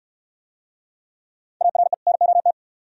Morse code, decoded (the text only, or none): ARE NOT